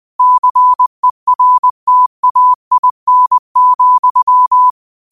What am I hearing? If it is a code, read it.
Morse code, heard as a keyed tone, about 20 wpm, 1000 Hz. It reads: CERTAIN,